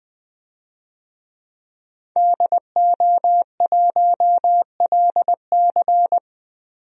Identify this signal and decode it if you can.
Morse code, heard as a keyed tone, about 20 wpm, 700 Hz. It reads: DO1LC
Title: DXCC-100 Call Signs - 5 Characters - Repeated 2x - 20wpm